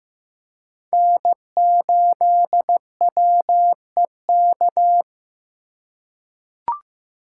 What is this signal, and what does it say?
Morse code, heard as a keyed tone, about 15 wpm, 700 Hz. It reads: N8WEK
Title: USA Call Signs - 3x Character Spacing - 15wpm